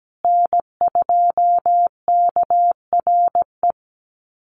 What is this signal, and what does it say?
Morse code, heard as a keyed tone, about 17 wpm, 700 Hz. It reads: N2KRE
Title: USA Call Signs Repeated 2x 17wpm